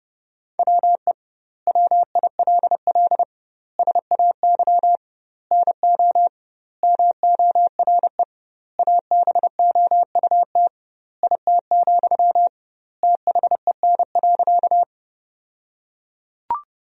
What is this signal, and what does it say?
Morse code, heard as a keyed tone, about 31 wpm, 700 Hz. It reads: WE WILL SAY NO MORE ABOUT IT, THEN.
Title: 2x Word-Spacing - Sentences from Top 200 Words - 30wpm